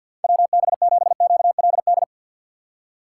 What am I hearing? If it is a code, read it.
Morse code, heard as a keyed tone, about 53 wpm, 700 Hz. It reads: WB7QLD
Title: USA Call Signs - 4x Character Spacing - 50wpm